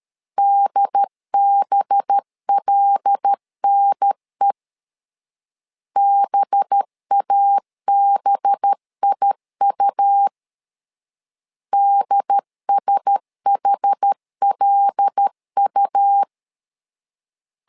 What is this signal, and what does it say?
Morse code, heard as a keyed tone, about 13 wpm, 800 Hz. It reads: DBLNE BABIU DSHLU
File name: LEZIONI / 5 LBVC / 5 LB 12wpm 1906